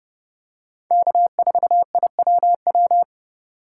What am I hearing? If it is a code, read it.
Morse code, heard as a keyed tone, about 30 wpm, 700 Hz. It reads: K4IWW
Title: USA Call Signs - 5 Characters - 3x Character Spacing - 30wpm